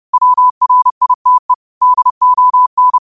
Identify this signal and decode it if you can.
Morse code, heard as a keyed tone, about 30 wpm, 1000 Hz. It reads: WRITE DON